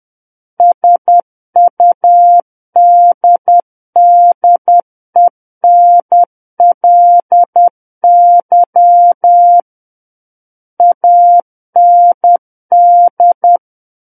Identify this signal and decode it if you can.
Morse code, heard as a keyed tone, about 10 wpm, 700 Hz. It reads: SUDDENLY AND
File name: Book One / 01-Chapter 1 - 10 WPM